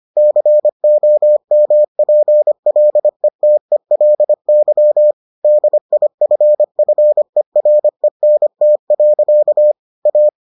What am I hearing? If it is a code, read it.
Morse code, heard as a keyed tone, about 25 wpm, 600 Hz. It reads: COMPLETELY DIFFERENT. A